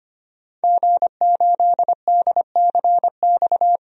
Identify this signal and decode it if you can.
Morse code, heard as a keyed tone, about 25 wpm, 700 Hz. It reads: G8DCX
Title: DXCC-100 Call Signs - 5 Characters - 25wpm